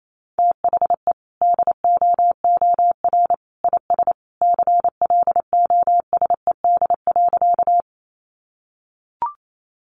Morse code, heard as a keyed tone, about 28 wpm, 700 Hz. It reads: THE DOOR IS CLOSED.
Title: Sentences from Top 1400 Words 28wpm